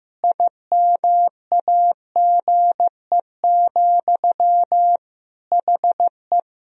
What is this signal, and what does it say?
Morse code, heard as a keyed tone, about 15 wpm, 700 Hz. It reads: IMAGE, HE
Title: Sentences from Top 1000 Words - Review Part 4 - 15wpm